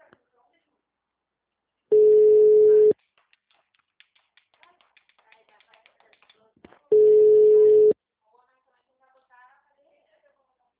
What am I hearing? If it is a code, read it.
Ringback tone (Europe, 425 Hz), with speech.
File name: out- -2003- - - .43229